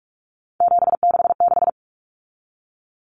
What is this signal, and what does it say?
Morse code, heard as a keyed tone, about 44 wpm, 700 Hz. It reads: 766
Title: Common Cut Numbers - 3-Digit 45wpm